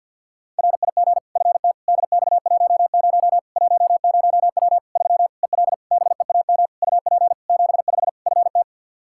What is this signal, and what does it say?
Morse code, heard as a keyed tone, about 51 wpm, 700 Hz. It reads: RIG FTDX10 10W 3 EL BEAM UP 75 FT
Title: K1USN SST QSO Practice 50wpm